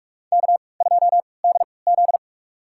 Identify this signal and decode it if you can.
Morse code, heard as a keyed tone, about 42 wpm, spaced as for 25 wpm, 700 Hz. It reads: K2DZ
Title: USA Call Signs - 4 Characters - 3x Character Spacing - 45wpm